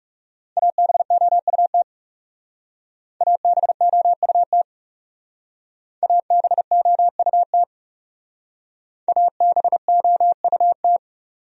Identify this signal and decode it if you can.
Morse code, heard as a keyed tone, about 38 wpm, 700 Hz. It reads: ABOUT ABOUT ABOUT ABOUT
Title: Speed-Racing Top 100 Words 30wpm